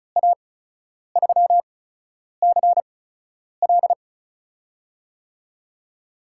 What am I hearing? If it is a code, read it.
Morse code, heard as a keyed tone, about 36 wpm, spaced as for 10 wpm, 700 Hz. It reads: A3CL
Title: USA Call Signs - 8x Character Spacing - 35wpm